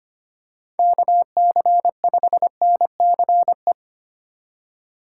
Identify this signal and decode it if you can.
Morse code, heard as a keyed tone, about 25 wpm, 700 Hz. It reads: KC5NCE
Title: USA Call Signs - 6 Characters - 3x Character Spacing - 25wpm